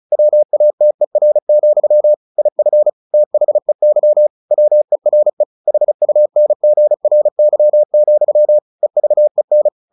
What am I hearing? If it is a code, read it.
Morse code, heard as a keyed tone, about 34 wpm, 600 Hz. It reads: WATER, IF THEY WERE HUNGRY, EVEN
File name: the old man and the sea 2 35wpm